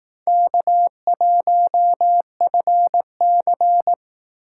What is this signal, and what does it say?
Morse code, heard as a keyed tone, about 18 wpm, 700 Hz. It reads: K1FC
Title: Speed Racing - USA 1x2 Call Signs 15wpm